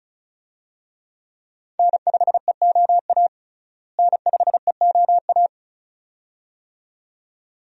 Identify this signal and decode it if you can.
Morse code, heard as a keyed tone, about 36 wpm, 700 Hz. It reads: N5EOA N5EOA
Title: USA Call Signs - 5 Characters - Repeated 2x - 35wpm